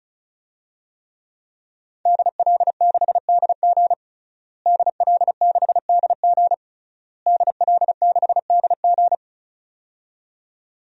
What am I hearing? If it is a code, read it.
Morse code, heard as a keyed tone, about 36 wpm, 690 Hz. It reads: DL6DG DL6DG DL6DG
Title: DXCC-100 Call Signs - 5 Characters - Repeated 3x - 35wpm